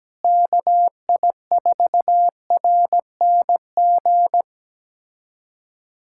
Morse code, heard as a keyed tone, about 17 wpm, 700 Hz. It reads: KI4RNG
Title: USA Callsign DE Callsign 17wpm